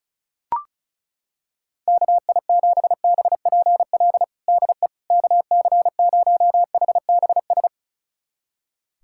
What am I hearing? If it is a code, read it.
Morse code, heard as a keyed tone, about 36 wpm, 700 Hz. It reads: KI7BPL DE KC0HBS
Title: USA Callsign DE Callsign - 6 Characters - 35wpm